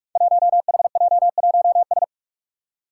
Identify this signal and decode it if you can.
Morse code, heard as a keyed tone, about 44 wpm, 700 Hz. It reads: 1HJ1S
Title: Five Challenging Letter-Number Combinations 45wpm